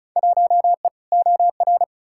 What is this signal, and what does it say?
Morse code, heard as a keyed tone, about 36 wpm, 700 Hz. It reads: 1E OR
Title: Field Day Exchanges Encore - 35wpm